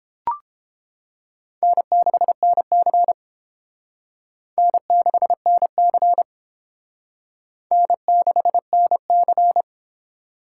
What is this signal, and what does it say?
Morse code, heard as a keyed tone, about 29 wpm, 700 Hz. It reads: N6NC N6NC N6NC